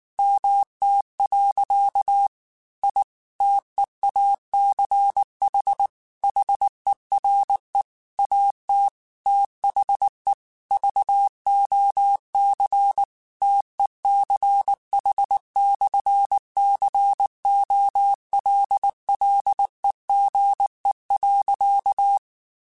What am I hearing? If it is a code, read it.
Morse code, heard as a keyed tone, about 19 wpm, 800 Hz. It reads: MT. I TEACH HERE AT THE VOC TECH/COLLEGE.
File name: QSOs / QSOs-mp3s / WD0BGP 1@18